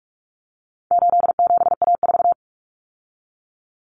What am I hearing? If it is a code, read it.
Morse code, heard as a keyed tone, about 44 wpm, 700 Hz. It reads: 87A4
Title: Common Cut Numbers - 4-Digit 45wpm